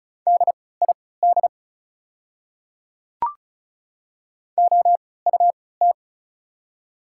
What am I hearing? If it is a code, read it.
Morse code, heard as a keyed tone, about 34 wpm, spaced as for 20 wpm, 700 Hz. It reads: DID OUT
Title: 3x Character Spacing - Top 100 Words - 35wpm